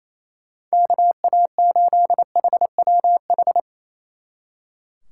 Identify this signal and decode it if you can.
Morse code, heard as a keyed tone, about 34 wpm, 700 Hz. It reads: KA8HWH
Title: Speed Racing - USA 2x3 Call Signs 28wpm